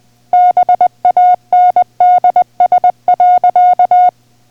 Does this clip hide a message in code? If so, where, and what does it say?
Yes: Morse code from 0.3 to 4.1 s, heard as a keyed tone, about 20 wpm, 720 Hz. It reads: BANDS.